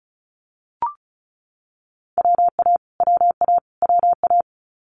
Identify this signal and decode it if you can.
Morse code, heard as a keyed tone, about 36 wpm, 700 Hz. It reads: WA WA WA
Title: Multi-letter Phonemes Repeated 3x 35wpm